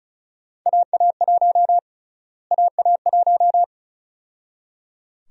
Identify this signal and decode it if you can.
Morse code, heard as a keyed tone, about 36 wpm, 700 Hz. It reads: AA1 AA1
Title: USA Call Sign Prefixes - Repeated 2x - 35wpm